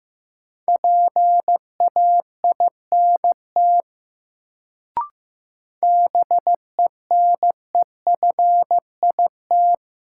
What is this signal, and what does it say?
Morse code, heard as a keyed tone, about 15 wpm, 700 Hz. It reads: PAINT BENEFIT
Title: Sentences from Top 1000 Words - Review Part 6 - New Word Drill - 15wpm